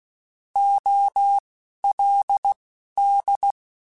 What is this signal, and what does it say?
Morse code, heard as a keyed tone, about 16 wpm, spaced as for 11 wpm, 800 Hz. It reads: OLD
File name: (spaced) / QSOs Spaced-mp3s / WD0FBT 1 @15 sp